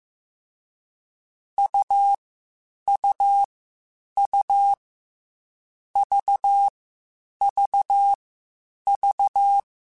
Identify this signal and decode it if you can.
Morse code, heard as a keyed tone, about 15 wpm, 800 Hz. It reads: U U U V V V